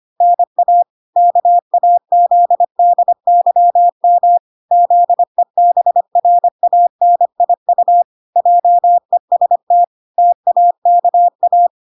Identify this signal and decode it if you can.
Morse code, heard as a keyed tone, about 25 wpm, 700 Hz. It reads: NA KAZDYM ZEBRANIU JEST TAKA